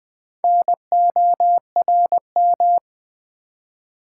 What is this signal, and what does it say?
Morse code, heard as a keyed tone, about 20 wpm, 700 Hz. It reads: NORM
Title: Build A Word - 2 Letters At A Time - 8-Letter Words - 20wpm